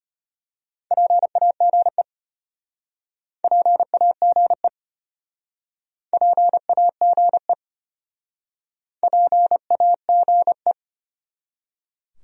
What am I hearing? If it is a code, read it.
Morse code, heard as a keyed tone, about 35 wpm, 700 Hz. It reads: PAGE PAGE PAGE PAGE